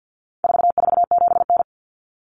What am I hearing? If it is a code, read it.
Morse code, heard as a keyed tone, about 50 wpm, 700 Hz. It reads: 447N
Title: Common Cut Numbers - 4-Digit 50wpm